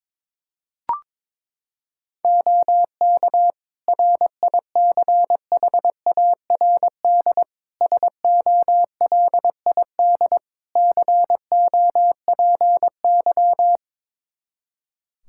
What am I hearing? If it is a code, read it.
Morse code, heard as a keyed tone, about 22 wpm, 700 Hz. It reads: OK RICHARD SOLID COPY
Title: Practice QSO - Exchange 3 - More Info - 22wpm